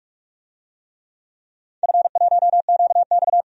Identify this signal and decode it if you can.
Morse code, heard as a keyed tone, about 46 wpm, 700 Hz. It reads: F1QX
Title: DXCC-100 Call Signs - 4 Characters - 45wpm